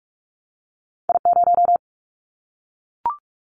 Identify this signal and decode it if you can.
Morse code, heard as a keyed tone, about 46 wpm, 700 Hz. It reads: I0
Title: Callsign Prefixes for Top 25 DXCC - 45wpm